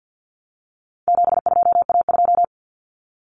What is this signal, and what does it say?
Morse code, heard as a keyed tone, about 50 wpm, 700 Hz. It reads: 72A2
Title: Common Cut Numbers - 4-Digit 50wpm